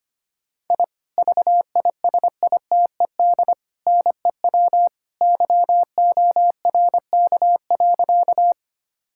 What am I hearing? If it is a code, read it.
Morse code, heard as a keyed tone, about 25 wpm, 700 Hz. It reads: I VISITED NEW YORK.
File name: practice / 25wpm / Sentences with 2-Word Phrases - 25wpm